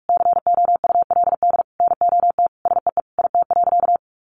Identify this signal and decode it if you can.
Morse code, heard as a keyed tone, about 44 wpm, 700 Hz. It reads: COULD NOT SEE IT.
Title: Sentences from Top 1000 Words 45wpm